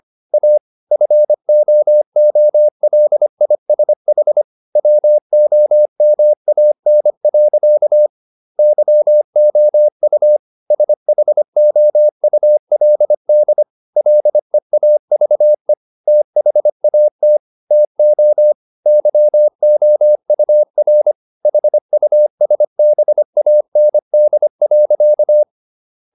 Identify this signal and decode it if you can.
Morse code, heard as a keyed tone, about 25 wpm, 600 Hz. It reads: A FOOLISH WOMAN. YOU SHOULD LEAVE THAT TO YOUR HUSBAND.